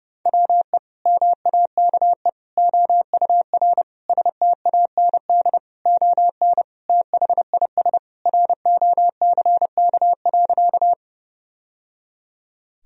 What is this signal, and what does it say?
Morse code, heard as a keyed tone, about 30 wpm, 700 Hz. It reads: WE MAKE OUR STAND ON THIS ROCK.
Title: Sentences from Top 1000 Words - Review Part 2 - 30wpm